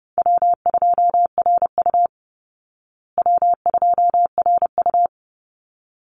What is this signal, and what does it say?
Morse code, heard as a keyed tone, about 30 wpm, 700 Hz. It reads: W2RU W2RU
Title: Call Signs Repeated 3x - 30wpm